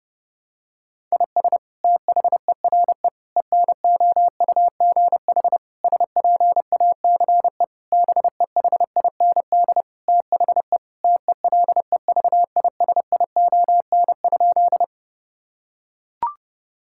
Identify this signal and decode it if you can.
Morse code, heard as a keyed tone, about 30 wpm, 700 Hz. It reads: IS THERE ENOUGH SPACE BEHIND THE TELEVISION?
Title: Sentences from Top 1000 Words - Review Part 5 - 30wpm